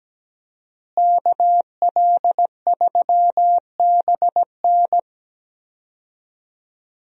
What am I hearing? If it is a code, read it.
Morse code, heard as a keyed tone, about 17 wpm, 700 Hz. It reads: KL3BN
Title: USA Call Signs - 5 Characters - Repeated 3x - 17wpm